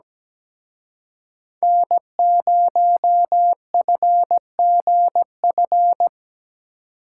Morse code, heard as a keyed tone, about 20 wpm, 700 Hz. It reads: N0FGF
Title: Speed Racing - USA 1x3 Call Signs 17wpm